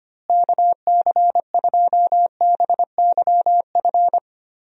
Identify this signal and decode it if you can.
Morse code, heard as a keyed tone, about 25 wpm, 700 Hz. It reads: KC2BYF
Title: USA Call Signs - 6 Characters - Repeated 3x - 25wpm